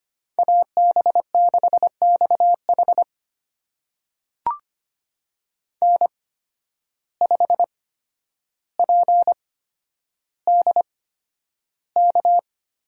Morse code, heard as a keyed tone, about 25 wpm, spaced as for 11 wpm, 700 Hz. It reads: AB6XH N5PDK
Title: USA Call Signs - 5 Characters - 8x Character Spacing - 25wpm